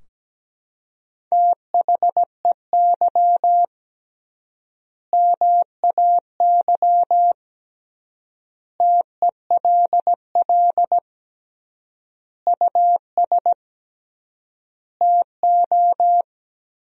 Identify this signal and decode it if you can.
Morse code, heard as a keyed tone, about 17 wpm, 700 Hz. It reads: THEY MAY TELL US TO
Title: 3x Word-Spacing - Sentences from Top 300 Words - 17wpm